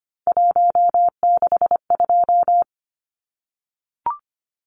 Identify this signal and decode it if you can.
Morse code, heard as a keyed tone, about 25 wpm, 700 Hz. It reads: 162
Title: Numbers - 3-Digit 25wpm